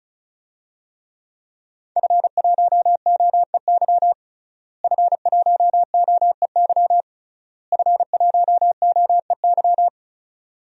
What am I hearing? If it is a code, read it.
Morse code, heard as a keyed tone, about 34 wpm, 700 Hz. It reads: F1OEY F1OEY F1OEY
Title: DXCC-100 Call Signs - 5 Characters - Repeated 3x - 35wpm